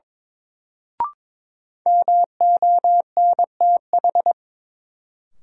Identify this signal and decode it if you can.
Morse code, heard as a keyed tone, about 22 wpm, 700 Hz. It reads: MONTH
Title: Sentences from Top 800 Words - New Word Drill - 22wpm